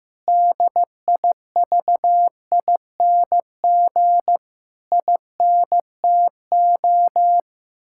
Morse code, heard as a keyed tone, about 15 wpm, 700 Hz. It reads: DIVING INTO
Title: Pangram Challenge v2 - 15wpm